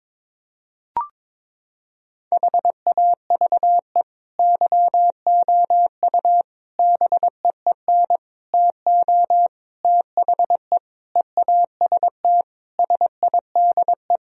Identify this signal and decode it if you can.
Morse code, heard as a keyed tone, about 21 wpm, 700 Hz. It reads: HAVE YOU BEEN TO THE EAST SIDE
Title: Sentences from Top 1000 Words - Review Part 3 - 22wpm